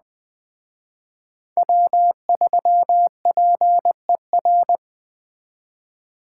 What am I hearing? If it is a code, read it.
Morse code, heard as a keyed tone, about 20 wpm, 700 Hz. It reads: W3PER
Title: Speed Racing - USA 1x3 Call Signs 17wpm